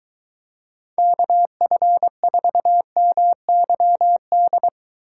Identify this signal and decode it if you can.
Morse code, heard as a keyed tone, about 23 wpm, 700 Hz. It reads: KF4MYD